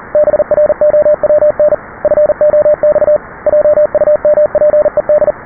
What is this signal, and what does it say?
Morse code, heard as a keyed tone, about 39 wpm, 600 Hz, with radio noise. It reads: BROWN FOX JUMPED
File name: test / test20db